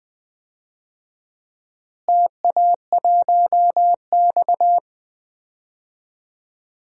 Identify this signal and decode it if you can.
Morse code, heard as a keyed tone, about 21 wpm, 700 Hz. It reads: TA1X
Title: DXCC-100 Call Signs - 4 Characters - Repeated 2x - 20wpm